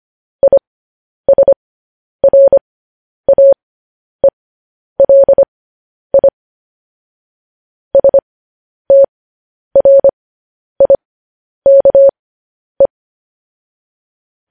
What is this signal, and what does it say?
Morse code, heard as a keyed tone, about 25 wpm, spaced as for 10 wpm, 550 Hz. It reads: ISRAELI STRIKE
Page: news-headlines-in-morse-code-at-10-wpm